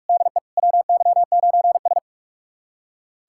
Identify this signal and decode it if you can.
Morse code, heard as a keyed tone, about 46 wpm, 700 Hz. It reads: DE WY9S
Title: USA Callsign DE Callsign - 4 Characters - 45wpm